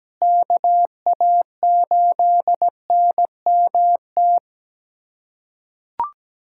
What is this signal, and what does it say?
Morse code, heard as a keyed tone, about 17 wpm, 700 Hz. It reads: KA8NMT
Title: USA Callsign DE Callsign - 6 Characters - 17wpm